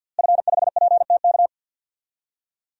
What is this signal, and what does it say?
Morse code, heard as a keyed tone, about 48 wpm, 700 Hz. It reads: F5PTK
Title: DXCC-100 Call Signs - 5 Characters - Repeated 2x - 50wpm